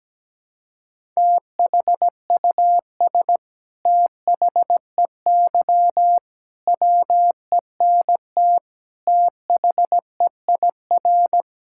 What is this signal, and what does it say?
Morse code, heard as a keyed tone, about 17 wpm, 700 Hz. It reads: THUS THEY WENT THEIR
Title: Sentences from Top 1000 Words - Review Part 5 - 17wpm